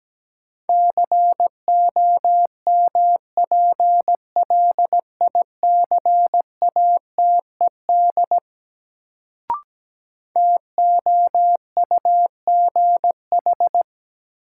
Morse code, heard as a keyed tone, about 17 wpm, 700 Hz. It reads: COMPLICATED TOUGH
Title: Sentences from Top 1200 Words - New Word Drill - 17wpm